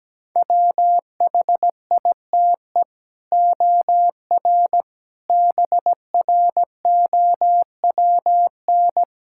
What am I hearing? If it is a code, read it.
Morse code, heard as a keyed tone, about 17 wpm, 700 Hz. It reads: WHITE OR BROWN
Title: Sentences from Top 1600 Words 17wpm